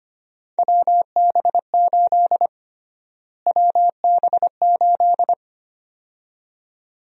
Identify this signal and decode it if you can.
Morse code, heard as a keyed tone, about 25 wpm, 700 Hz. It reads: WB8 WB8
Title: USA Call Sign Prefixes - Repeated 2x - 25wpm